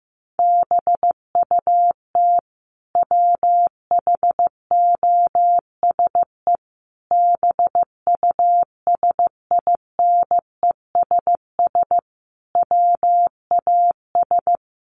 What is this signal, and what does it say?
Morse code, heard as a keyed tone, about 15 wpm, 700 Hz. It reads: BUT WHOSE BUSINESS WAS
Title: Sentences from Top 1500 Words 15wpm